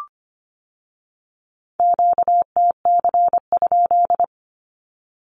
Morse code, heard as a keyed tone, about 25 wpm, 700 Hz. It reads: QTC?
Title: Q-Codes with Spoken Meaning - 25wpm